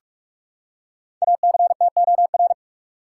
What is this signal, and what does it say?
Morse code, heard as a keyed tone, about 46 wpm, 700 Hz. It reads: ACTOR
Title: Five Letter Words - 45wpm